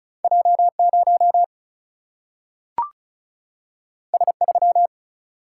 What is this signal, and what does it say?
Morse code, heard as a keyed tone, about 36 wpm, 700 Hz. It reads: J0 S3
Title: Single-Letter Single-Number 35wpm